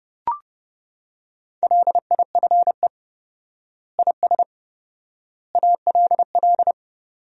Morse code, heard as a keyed tone, about 30 wpm, 700 Hz. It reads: LIFE IS ALL
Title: 4x Word-Spacing - Sentences from Top 300 Words - 30wpm